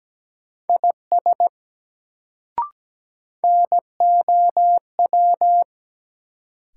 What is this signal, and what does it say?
Morse code, heard as a keyed tone, about 17 wpm, 700 Hz. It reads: IS NOW
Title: Sets of 2 Words - New Word Drill - 17wpm